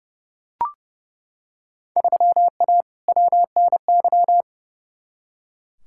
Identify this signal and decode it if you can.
Morse code, heard as a keyed tone, about 30 wpm, 700 Hz. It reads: 3A WNY
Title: Field Day Exchanges - 30wpm